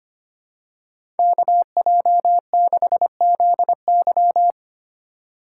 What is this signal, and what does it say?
Morse code, heard as a keyed tone, about 25 wpm, 700 Hz. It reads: KJ6ZY
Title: USA Call Signs - 5 Characters - 2x Character Spacing - 25wpm